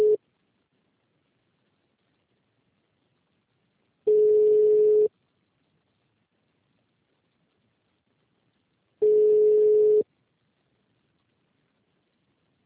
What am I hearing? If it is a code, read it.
Ringback tone (Europe, 425 Hz).